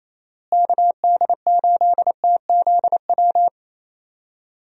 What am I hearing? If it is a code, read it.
Morse code, heard as a keyed tone, about 28 wpm, 700 Hz. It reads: KD8TZW
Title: USA Call Signs - 6 Characters - Repeated 3x - 28wpm